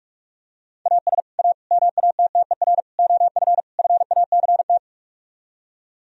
Morse code, heard as a keyed tone, about 45 wpm, 700 Hz. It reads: AS A MATTER OF FACT